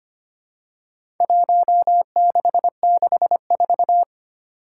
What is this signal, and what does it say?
Morse code, heard as a keyed tone, about 25 wpm, 700 Hz. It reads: 1664
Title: Four Challenging Letter-Number Combinations 25wpm